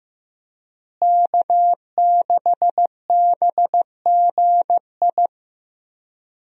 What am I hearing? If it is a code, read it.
Morse code, heard as a keyed tone, about 15 wpm, 700 Hz. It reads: K6BGI
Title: USA Call Signs - 5 Characters - Repeated 2x - 15wpm